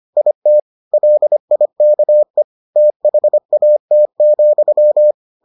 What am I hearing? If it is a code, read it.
Morse code, heard as a keyed tone, about 25 wpm, 600 Hz. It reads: IT LIKE THAT,